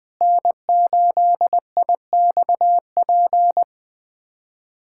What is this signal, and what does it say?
Morse code, heard as a keyed tone, about 21 wpm, 700 Hz. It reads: N8IXP
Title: USA Callsign DE Callsign - 5 Characters - 20wpm